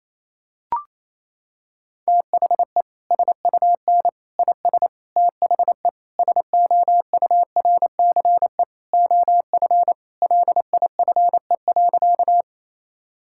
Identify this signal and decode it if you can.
Morse code, heard as a keyed tone, about 28 wpm, 700 Hz. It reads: THE SUN IS THE SOURCE OF LIFE.
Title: Sentences from Top 1000 Words - Review Part 5 - 28wpm